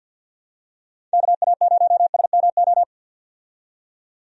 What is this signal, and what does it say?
Morse code, heard as a keyed tone, about 51 wpm, 700 Hz. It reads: KA0SMO
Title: USA Call Signs - 6 Characters - 50wpm